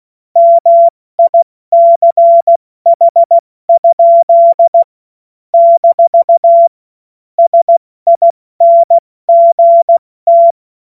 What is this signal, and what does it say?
Morse code, heard as a keyed tone, about 16 wpm, 680 Hz. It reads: MICH? - SINGT
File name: Morsekurs / Witze / Witz 24. August 2021, 70 F80